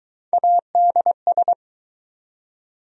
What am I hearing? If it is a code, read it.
Morse code, heard as a keyed tone, about 20 wpm, 700 Hz. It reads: ADS